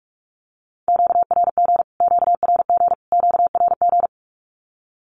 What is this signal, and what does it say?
Morse code, heard as a keyed tone, about 44 wpm, 700 Hz. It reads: QRG QRG QRG
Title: Q-Codes Repeated 3x 45wpm